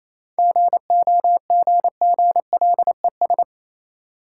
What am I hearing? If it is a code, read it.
Morse code, heard as a keyed tone, about 28 wpm, 690 Hz. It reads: GOGGLES